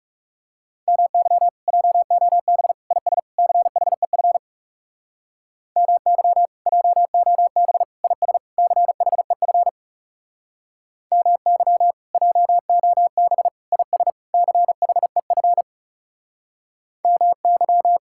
Morse code, heard as a keyed tone, about 40 wpm, 700 Hz. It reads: MY JOB IS CHEF MY JOB IS CHEF MY JOB IS CHEF MY